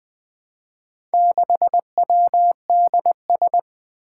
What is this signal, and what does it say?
Morse code, heard as a keyed tone, about 20 wpm, 700 Hz. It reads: 6WDS
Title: Four Challenging Letter-Number Combinations 20wpm